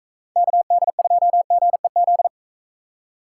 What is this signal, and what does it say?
Morse code, heard as a keyed tone, about 42 wpm, 700 Hz. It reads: KD2GEZ